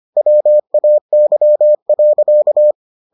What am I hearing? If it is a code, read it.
Morse code, heard as a keyed tone, about 25 wpm, 600 Hz. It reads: WAY.